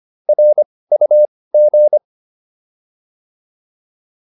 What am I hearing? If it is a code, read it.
Morse code, heard as a keyed tone, about 25 wpm, spaced as for 18 wpm, 600 Hz. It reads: RUG